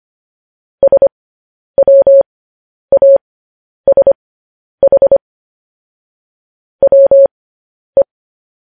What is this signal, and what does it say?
Morse code, heard as a keyed tone, about 25 wpm, spaced as for 10 wpm, 550 Hz. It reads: SWASH WE
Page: news-headlines-in-morse-code-at-10-wpm